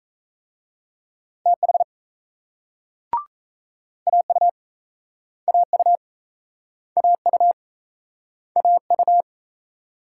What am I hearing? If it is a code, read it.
Morse code, heard as a keyed tone, about 37 wpm, 700 Hz. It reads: TH AU AU AU AU